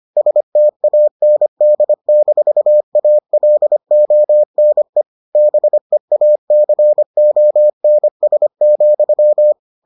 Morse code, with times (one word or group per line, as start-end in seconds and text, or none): 0.2-5.0 STAND-ALONE
5.4-9.5 BEACONS,